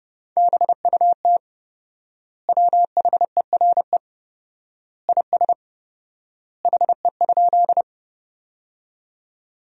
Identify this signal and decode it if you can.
Morse code, heard as a keyed tone, about 30 wpm, 700 Hz. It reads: BUT WHERE IS HE?
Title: 4x Word-Spacing - Sentences from Top 200 Words - 30wpm